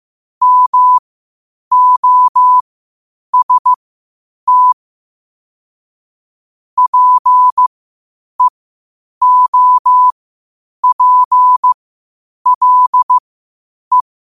Morse code, heard as a keyed tone, about 15 wpm, spaced as for 8 wpm, 1000 Hz. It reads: MOST PEOPLE
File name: qotdfiles / 8wpm